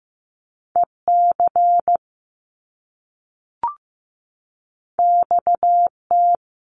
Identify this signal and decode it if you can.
Morse code, heard as a keyed tone, about 15 wpm, 700 Hz. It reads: EC XT
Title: Top Two Letter Combinations in Words 15wpm